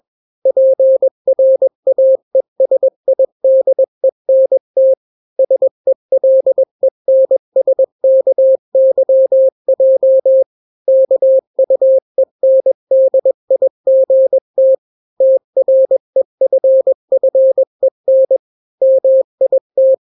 Morse code, heard as a keyed tone, about 21 wpm, 530 Hz. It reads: PRAESIDENT SELENSKYJ KUENDIGT TREFFEN MIT
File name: daily / 20wpm